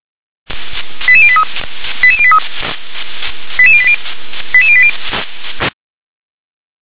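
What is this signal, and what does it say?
Five-tone selective call (ZVEI), heard as quick runs of five tones, tones of about 70 ms. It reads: 88083 88083 88080 88080
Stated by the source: ZVEI five-tone selective call received on an SDR.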